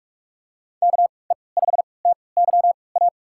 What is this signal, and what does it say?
Morse code, heard as a keyed tone, about 41 wpm, spaced as for 25 wpm, 700 Hz. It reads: KE5TYA
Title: USA Call Signs - 6 Characters - 3x Character Spacing - 45wpm